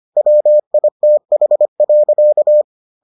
Morse code, heard as a keyed tone, about 25 wpm, 600 Hz. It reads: WITH.